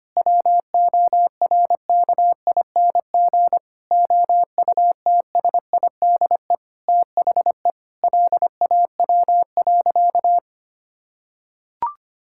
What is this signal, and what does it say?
Morse code, heard as a keyed tone, about 25 wpm, 700 Hz. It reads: WORKING OUTSIDE THE LAW.
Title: Sentences from Top 1000 Words - Review Part 6 - 25wpm